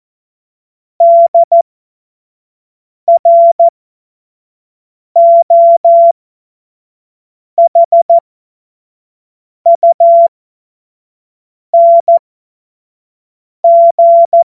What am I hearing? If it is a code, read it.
Morse code, heard as a keyed tone, about 14 wpm, spaced as for 5 wpm, 680 Hz. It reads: DROHUNG
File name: Morsekurs / Witze / Witz 10. Januar 2023 25 F70